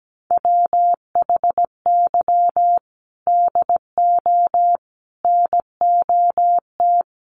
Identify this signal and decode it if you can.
Morse code, heard as a keyed tone, about 17 wpm, 700 Hz. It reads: WHY DO NOT
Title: Sets of 4 Words 17wpm